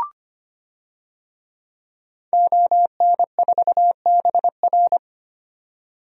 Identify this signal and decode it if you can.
Morse code, heard as a keyed tone, about 25 wpm, 700 Hz. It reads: ON4BR
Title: DXCC-100 Call Signs - 5 Characters - 25wpm